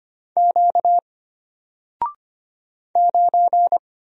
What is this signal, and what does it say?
Morse code, heard as a keyed tone, about 25 wpm, 700 Hz. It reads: Q 9
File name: Single-Letter-Number-25wpm